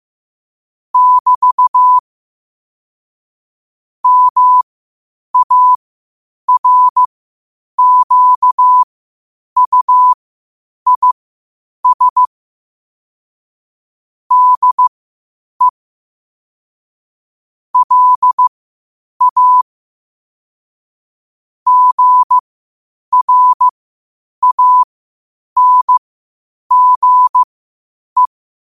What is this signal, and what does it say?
Morse code, heard as a keyed tone, about 15 wpm, spaced as for 8 wpm, 1000 Hz. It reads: <BT> MARQUIS DE LA GRANGE
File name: qotdfiles / 8wpm